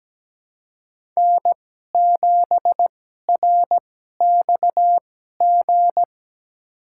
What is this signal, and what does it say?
Morse code, heard as a keyed tone, about 17 wpm, spaced as for 12 wpm, 700 Hz. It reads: N7RXG
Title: USA Call Signs - 5 Characters - 2x Character Spacing - 17wpm